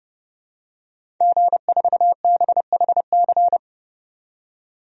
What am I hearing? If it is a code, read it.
Morse code, heard as a keyed tone, about 32 wpm, 700 Hz. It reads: G4BHC
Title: DXCC-100 Call Signs - 5 Characters - 30wpm